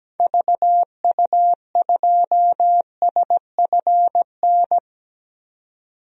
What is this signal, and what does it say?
Morse code, heard as a keyed tone, about 17 wpm, 700 Hz. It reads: VU2SFN
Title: DXCC-100 Call Signs - 6 Characters - 17wpm